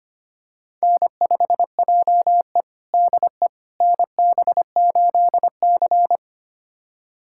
Morse code, heard as a keyed tone, about 25 wpm, 700 Hz. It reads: N5JE DE NB8C
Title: USA Callsign DE Callsign - 4 Characters - 25wpm